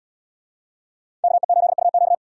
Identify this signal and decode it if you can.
Morse code, heard as a keyed tone, about 73 wpm, 700 Hz. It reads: KE8UQ